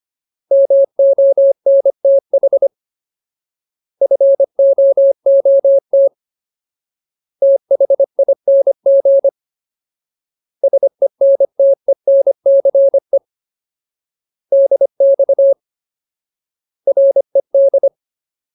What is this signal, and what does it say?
Morse code, heard as a keyed tone, about 25 wpm, 550 Hz. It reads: MONTH FOOT THING SENTENCE DX RED
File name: Ultimate Word List / 25WPM Ultimate